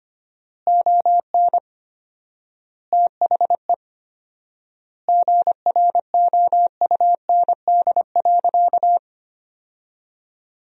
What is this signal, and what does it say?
Morse code, heard as a keyed tone, about 25 wpm, 700 Hz. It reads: ON THE GROUND.
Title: 4x Word-Spacing - Sentences from Top 500 Words - 25wpm